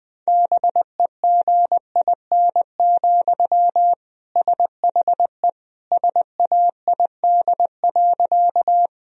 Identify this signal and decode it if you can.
Morse code, heard as a keyed tone, about 20 wpm, 700 Hz. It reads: BEGIN, SHE SAID.
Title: Sentences from Top 1000 Words - Review Part 1 - 20wpm